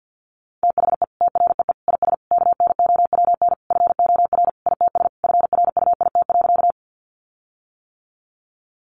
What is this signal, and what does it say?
Morse code, heard as a keyed tone, about 48 wpm, 700 Hz. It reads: THE TREE IS KNOWN FOR ITS FRUIT.
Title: Sentences from Top 1300 Words 50wpm